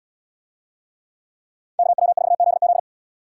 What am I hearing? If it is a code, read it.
Morse code, heard as a keyed tone, about 67 wpm, 700 Hz. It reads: KW4ZZ